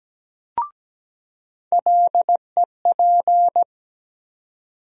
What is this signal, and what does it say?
Morse code, heard as a keyed tone, about 17 wpm, 700 Hz. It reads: LEP
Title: USA Call Sign Postfixes - 3 Letters 17wpm